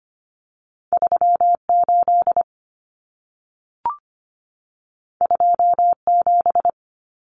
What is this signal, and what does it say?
Morse code, heard as a keyed tone, about 25 wpm, 700 Hz. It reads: 38 27
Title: Numbers - 2-Digit 25wpm